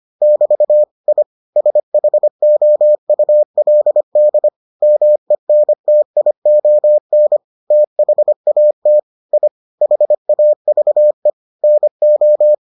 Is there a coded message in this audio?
Yes: Morse code from 0.2 to 12.5 s, heard as a keyed tone, about 25 wpm, 600 Hz. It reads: <BT> I SHOULD MENTION THAT I HAVE NO